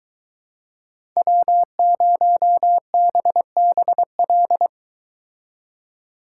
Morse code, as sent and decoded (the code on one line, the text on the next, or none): .-- ----- -... -... .-..
W0BBL